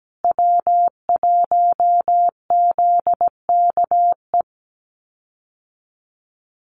Morse code, heard as a keyed tone, about 17 wpm, 700 Hz. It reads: W1ZKE
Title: USA Call Signs Repeated 3x 17wpm